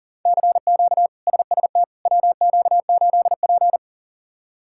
Morse code, heard as a keyed tone, about 40 wpm, 700 Hz. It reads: CQ SST WQ8P
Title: K1USN SST QSO Practice - 40wpm